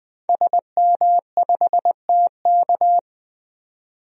Morse code, heard as a keyed tone, about 20 wpm, 700 Hz. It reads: SM5TK
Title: DXCC-100 Call Signs - 5 Characters - 20wpm